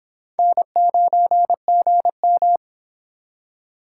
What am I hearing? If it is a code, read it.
Morse code, heard as a keyed tone, about 25 wpm, 700 Hz. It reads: N9GM